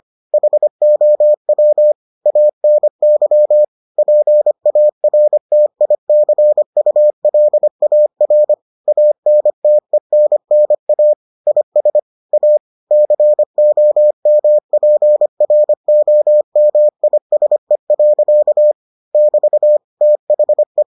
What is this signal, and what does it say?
Morse code, heard as a keyed tone, about 25 wpm, 600 Hz. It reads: HOW ANY PARTICULAR ANTENNA IS A COMPROMISE. <BT> THE